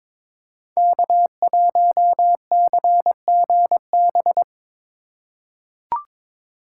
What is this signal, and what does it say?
Morse code, heard as a keyed tone, about 22 wpm, 700 Hz. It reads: K1CGB
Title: Code Groups - 5 Letter:Numbers - 22wpm